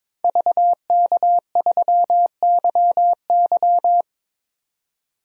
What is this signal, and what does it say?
Morse code, heard as a keyed tone, about 22 wpm, 700 Hz. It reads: VK3YY
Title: DXCC-100 Call Signs - 5 Characters - 22wpm